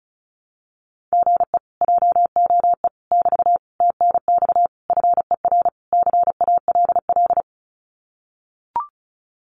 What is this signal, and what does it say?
Morse code, heard as a keyed tone, about 34 wpm, 700 Hz. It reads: GE JOE <BT> TNX FER CALL
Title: Common QSO Phrases Encore 35wpm